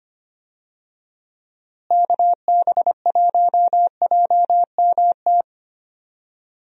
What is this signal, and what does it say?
Morse code, heard as a keyed tone, about 25 wpm, 700 Hz. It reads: KB1JMT
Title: USA Call Signs - 6 Characters - Repeated 3x - 25wpm